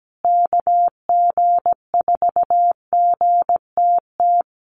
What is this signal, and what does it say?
Morse code, heard as a keyed tone, about 17 wpm, 700 Hz. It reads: KG4GTT
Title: USA Call Signs Repeated 2x 17wpm